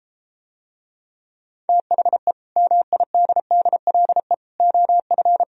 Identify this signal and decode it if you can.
Morse code, heard as a keyed tone, about 33 wpm, 700 Hz. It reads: THE MIDDLE OF